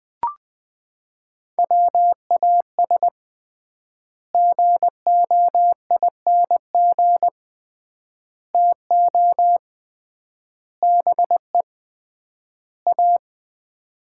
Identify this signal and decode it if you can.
Morse code, heard as a keyed tone, about 21 wpm, 690 Hz. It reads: WAS GOING TO BE A
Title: 3x Word-Spacing - Sets of 5 Words - 20wpm